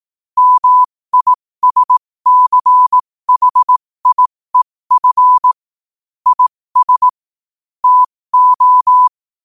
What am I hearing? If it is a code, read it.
Morse code, heard as a keyed tone, about 18 wpm, spaced as for 15 wpm, 1000 Hz. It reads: MISCHIEF IS TO
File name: qotdfiles / 15wpm